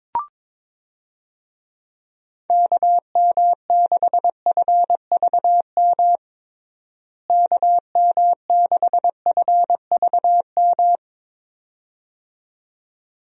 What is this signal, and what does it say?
Morse code, heard as a keyed tone, about 22 wpm, 700 Hz. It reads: KM6FVM KM6FVM
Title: USA Call Signs - 6 Characters - Repeated 2x - 22wpm